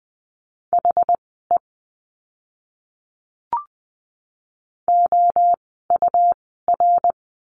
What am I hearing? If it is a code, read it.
Morse code, heard as a keyed tone, about 20 wpm, spaced as for 15 wpm, 700 Hz. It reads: HE OUR
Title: 139 Lesson 23f - Letter f - Words 20wpm